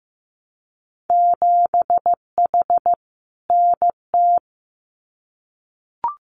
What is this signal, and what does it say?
Morse code, heard as a keyed tone, about 15 wpm, 700 Hz. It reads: 7H NT
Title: Winter Field Day Exchanges Encore 15wpm